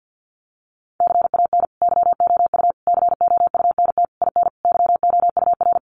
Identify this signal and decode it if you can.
Morse code, heard as a keyed tone, about 50 wpm, 700 Hz. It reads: CAN YOU COUNT IN YOUR
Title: Sentences from Top 1500 Words 50wpm